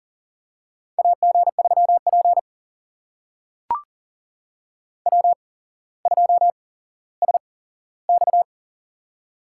Morse code, heard as a keyed tone, about 40 wpm, spaced as for 17 wpm, 700 Hz. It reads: AG3P W2SX
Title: USA Call Signs - 4 Characters - 8x Character Spacing - 40wpm